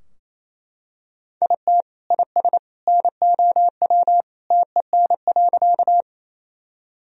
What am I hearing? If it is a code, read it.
Morse code, heard as a keyed tone, about 28 wpm, 700 Hz. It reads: IT IS NOW TEN.
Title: Sentences from Top 1000 Words - Review Part 2 - 28wpm